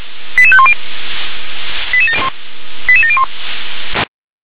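Five-tone selective call (ZVEI), heard as quick runs of five tones, tones of about 70 ms. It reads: 88510 88701
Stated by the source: ZVEI five-tone selective call 88510, received with an SDR.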